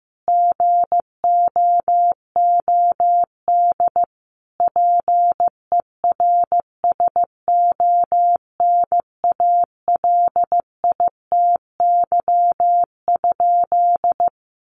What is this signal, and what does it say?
Morse code, heard as a keyed tone, about 15 wpm, 700 Hz. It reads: GOOD PERSONALITY?
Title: Sentences from Top 1400 Words 15wpm